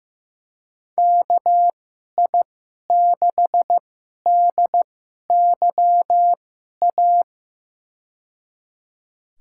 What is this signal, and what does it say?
Morse code, heard as a keyed tone, about 15 wpm, spaced as for 11 wpm, 700 Hz. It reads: KI6DYA
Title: USA Call Signs - 2x Character Spacing - 15wpm